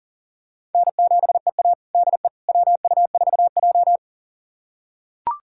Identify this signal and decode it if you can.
Morse code, heard as a keyed tone, about 40 wpm, 700 Hz. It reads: N7EA DE WU4J
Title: USA Callsign DE Callsign - 4 Characters - 40wpm